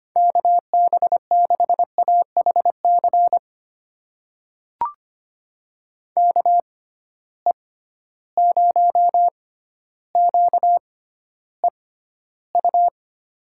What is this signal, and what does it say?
Morse code, heard as a keyed tone, about 25 wpm, spaced as for 13 wpm, 700 Hz. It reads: KB6AHC KE0QEU
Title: USA Call Signs - 6 Characters - 6x Character Spacing - 25wpm